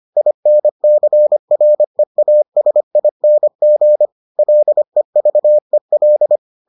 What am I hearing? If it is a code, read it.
Morse code, heard as a keyed tone, about 25 wpm, 600 Hz. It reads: INCREASING LEVEL